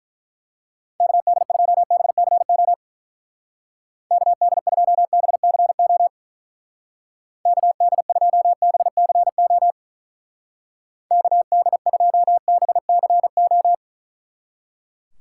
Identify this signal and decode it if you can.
Morse code, heard as a keyed tone, about 46 wpm, 700 Hz. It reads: KD2BCO KD2BCO KD2BCO KD2BCO